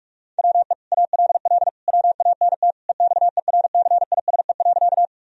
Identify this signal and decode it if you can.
Morse code, heard as a keyed tone, about 41 wpm, 700 Hz. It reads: WE ALL WANT EXERCISE.
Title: Sentences from Top 1000 Words - Review Part 4 - 45wpm